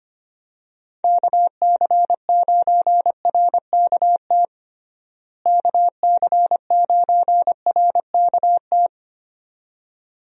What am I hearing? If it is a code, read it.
Morse code, heard as a keyed tone, about 25 wpm, 700 Hz. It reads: KC9RKT KC9RKT
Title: USA Call Signs - 6 Characters - Repeated 2x - 25wpm